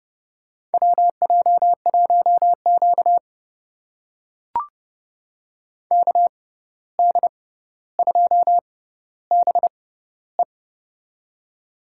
Morse code, heard as a keyed tone, about 32 wpm, spaced as for 15 wpm, 700 Hz. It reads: WJ1Q KD2BE
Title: USA Call Signs - 6x Character Spacing - 30wpm